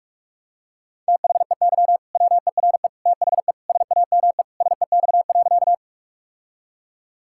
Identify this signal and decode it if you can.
Morse code, heard as a keyed tone, about 47 wpm, 700 Hz. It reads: THEY WERE THE SAME SEX.
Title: Sentences from Top 1600 Words 45wpm